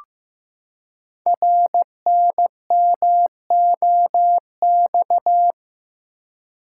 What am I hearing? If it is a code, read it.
Morse code, heard as a keyed tone, about 15 wpm, 700 Hz. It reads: RNMOX
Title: Code Groups - 5 Letters - 15wpm